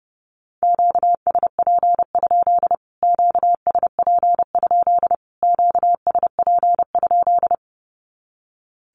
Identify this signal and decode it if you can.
Morse code, heard as a keyed tone, about 29 wpm, 700 Hz. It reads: QSP? QSP? QSP?
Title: Q-Codes Repeated 3x 30wpm